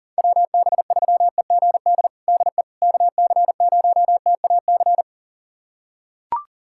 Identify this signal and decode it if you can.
Morse code, heard as a keyed tone, about 37 wpm, 700 Hz. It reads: WB3EGD DE KC0TAC
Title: USA Callsign DE Callsign - 6 Characters - 40wpm